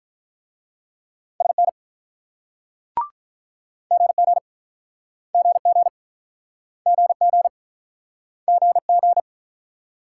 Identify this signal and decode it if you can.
Morse code, heard as a keyed tone, about 48 wpm, 700 Hz. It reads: SN GG GG GG GG